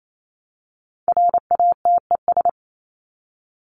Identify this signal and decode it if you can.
Morse code, heard as a keyed tone, about 28 wpm, 700 Hz. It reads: RATES
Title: ES:S Suffix - Top 100 Words with Base Words 28wpm